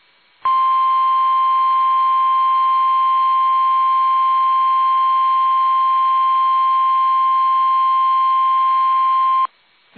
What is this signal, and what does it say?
Emergency Alert System (SAME) tones (the 1050 Hz attention tone). It is heard from 0.4 to 9.5 s.
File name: eas / KUSO / KUSO